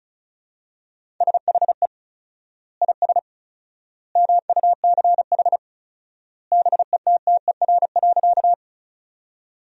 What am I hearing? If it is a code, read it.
Morse code, heard as a keyed tone, about 36 wpm, 700 Hz. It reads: SHE IS MUCH BETTER.
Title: 4x Word-Spacing - Sentences from Top 400 Words - 35wpm